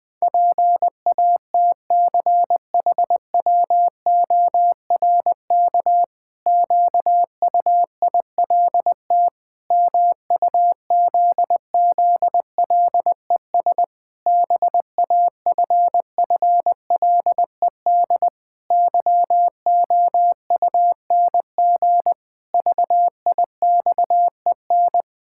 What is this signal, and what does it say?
Morse code, heard as a keyed tone, about 20 wpm, 700 Hz. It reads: PATCHWORK QUILT MUZZLES BAFFLED YOUNG VIXEN
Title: Pangram Challenge v2 - 20wpm